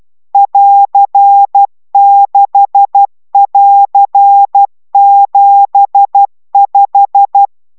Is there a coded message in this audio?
Yes: Morse code from 0.3 to 7.5 s, heard as a keyed tone, about 12 wpm, 800 Hz. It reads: <AR>6<AR>75